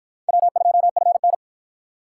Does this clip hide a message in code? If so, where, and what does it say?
Yes: Morse code from 0.3 to 1.3 s, heard as a keyed tone, about 53 wpm, 700 Hz. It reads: W2FN